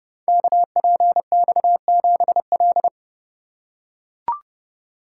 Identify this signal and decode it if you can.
Morse code, heard as a keyed tone, about 30 wpm, 700 Hz. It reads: KPX7L
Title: Code Groups - 5 Letter:Numbers - 30wpm